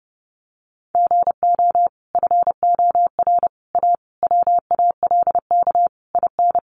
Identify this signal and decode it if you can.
Morse code, heard as a keyed tone, about 30 wpm, 700 Hz. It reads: GO FOR A WALK IN
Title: Sentences from Top 1300 Words 30wpm